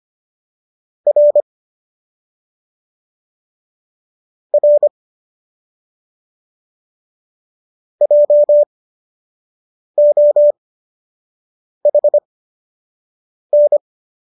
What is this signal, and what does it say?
Morse code, heard as a keyed tone, about 25 wpm, spaced as for 6 wpm, 590 Hz. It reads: R R JOHN